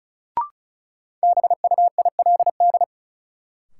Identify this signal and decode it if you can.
Morse code, heard as a keyed tone, about 36 wpm, 700 Hz. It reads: BUILD
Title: Sentences from Top 1000 Words - Review Part 1 - New Word Drill - 35wpm